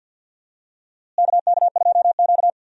Morse code, heard as a keyed tone, about 53 wpm, 700 Hz. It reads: KK2Q
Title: USA Call Signs - 4 Characters - Repeated 3x - 50wpm